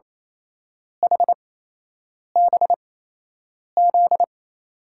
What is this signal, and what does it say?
Morse code, heard as a keyed tone, about 28 wpm, spaced as for 8 wpm, 700 Hz. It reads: HBZ
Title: USA Call Signs - 6 Characters - 8x Character Spacing - 28wpm